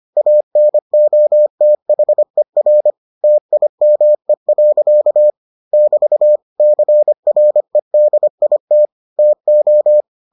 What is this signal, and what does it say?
Morse code, heard as a keyed tone, about 25 wpm, 600 Hz. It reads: ANOTHER TIME. <BT> CREDIT TO